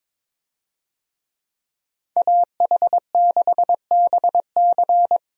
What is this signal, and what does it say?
Morse code, heard as a keyed tone, about 22 wpm, 700 Hz. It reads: AH6BC
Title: USA Call Signs - 5 Characters - 22wpm